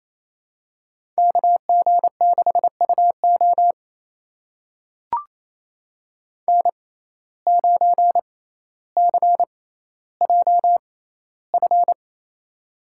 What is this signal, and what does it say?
Morse code, heard as a keyed tone, about 28 wpm, spaced as for 14 wpm, 700 Hz. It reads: KG6UO N9CJF
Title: USA Call Signs - 5 Characters - 6x Character Spacing - 28wpm